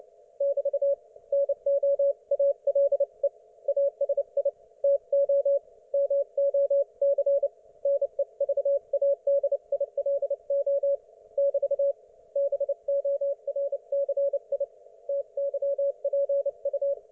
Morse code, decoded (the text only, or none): <BT> NOALE ASI TO MOC NEVADILO <BT> BORCI TYPU